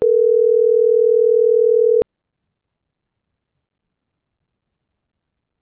Ringback tone (North America), 440 + 480 Hz.